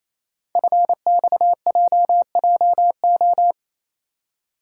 Morse code, heard as a keyed tone, about 27 wpm, 700 Hz. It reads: FXJJO
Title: Code Groups - 5 Letters - 28wpm